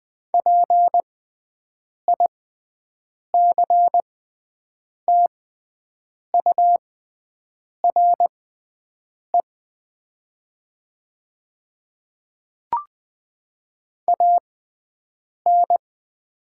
Morse code, heard as a keyed tone, about 20 wpm, spaced as for 7 wpm, 700 Hz. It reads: PICTURE AN
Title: 6x Character Spacing - Top 300 Words - 20wpm